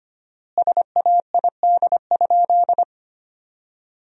Morse code, heard as a keyed tone, about 25 wpm, 700 Hz. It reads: SAID?
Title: 4x Word-Spacing - Sentences from Top 100 Words - 25wpm